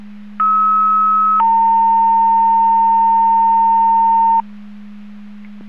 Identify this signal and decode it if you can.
Two-tone page, heard as a one-second tone, then a three-second tone, about 1290 then 910 Hz. It is heard from 0.4 to 4.4 s.